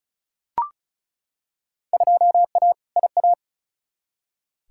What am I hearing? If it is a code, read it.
Morse code, heard as a keyed tone, about 36 wpm, 700 Hz. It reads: 2A IA
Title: Field Day Exchanges - 35wpm